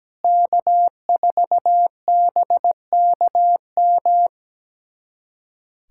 Morse code, heard as a keyed tone, about 17 wpm, 700 Hz. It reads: K4BKM
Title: USA Callsign DE Callsign - 5 Characters - 17wpm